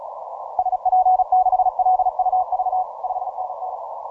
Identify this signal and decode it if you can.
Morse code, heard as a keyed tone, about 36 wpm, 740 Hz. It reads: SP6LUV 5NN
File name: simple-file-list / C92RU 12MCW